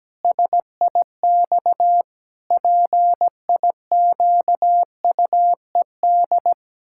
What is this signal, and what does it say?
Morse code, heard as a keyed tone, about 17 wpm, 700 Hz. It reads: SIX PIQUED
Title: Pangram Challenge v2 - 17wpm